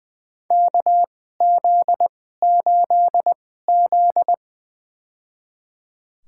Morse code, heard as a keyed tone, about 20 wpm, spaced as for 14 wpm, 700 Hz. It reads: KZ8Z
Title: USA Call Signs - 4 Characters - 2x Character Spacing - 20wpm